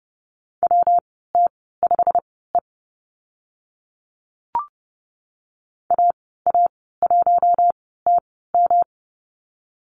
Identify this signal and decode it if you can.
Morse code, heard as a keyed tone, about 30 wpm, spaced as for 17 wpm, 700 Hz. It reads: WT5E AA1TM
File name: Lesson 19g - Letter w - Call Signs 30wpm